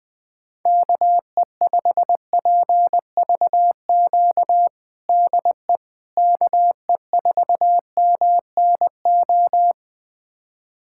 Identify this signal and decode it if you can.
Morse code, heard as a keyed tone, about 21 wpm, 700 Hz. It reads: KE5PVQ DE KE4MNO
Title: USA Callsign DE Callsign 20wpm